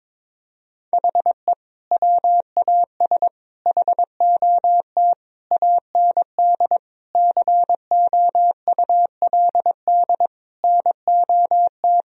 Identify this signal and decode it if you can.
Morse code, heard as a keyed tone, about 22 wpm, 700 Hz. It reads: HE WAS HOT AND COULD NOT
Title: Sentences from Top 1000 Words - Review Part 3 - 22wpm